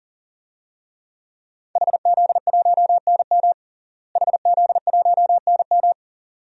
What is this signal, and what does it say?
Morse code, heard as a keyed tone, about 40 wpm, 700 Hz. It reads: HZ1NM HZ1NM
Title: DXCC-100 Call Signs - 5 Characters - Repeated 2x - 40wpm